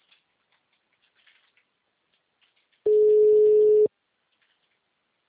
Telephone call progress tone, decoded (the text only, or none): ringback tone